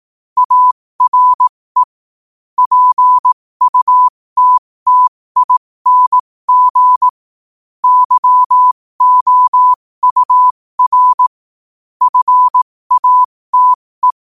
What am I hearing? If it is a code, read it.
Morse code, heard as a keyed tone, about 18 wpm, spaced as for 15 wpm, 1000 Hz. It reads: ARE PUTTING YOUR FATE